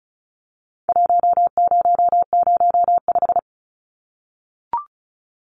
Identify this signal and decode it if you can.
Morse code, heard as a keyed tone, about 34 wpm, 700 Hz. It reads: 1005
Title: Numbers - Contest Style - 4-Digit 35wpm